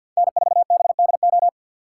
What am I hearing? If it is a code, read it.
Morse code, heard as a keyed tone, about 49 wpm, 700 Hz. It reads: N4BDO
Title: USA Call Signs - 4x Character Spacing - 50wpm